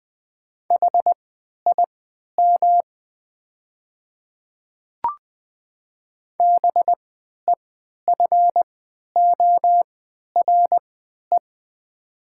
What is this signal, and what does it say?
Morse code, heard as a keyed tone, about 20 wpm, spaced as for 11 wpm, 700 Hz. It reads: HIM BEFORE
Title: 3x Character Spacing - Top 200 Words - 20wpm